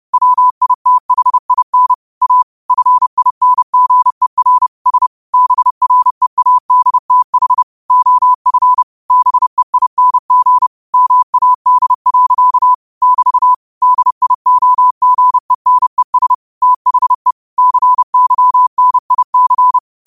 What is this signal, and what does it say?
Morse code, heard as a keyed tone, about 30 wpm, 1000 Hz. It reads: WITHIN A FINGER S BREADTH OF BEING MAD. <BT> DIOGENES THE CYNIC